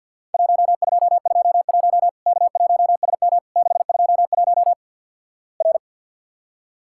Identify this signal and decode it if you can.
Morse code, heard as a keyed tone, about 50 wpm, 700 Hz. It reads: 1221 K1SM 621 R
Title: Worked All Europe DX Contest - QTC Exchange - 50wpm